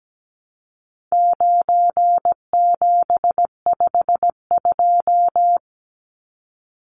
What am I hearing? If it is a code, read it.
Morse code, heard as a keyed tone, about 17 wpm, 700 Hz. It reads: 9752
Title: Numbers - 4-Digit 17wpm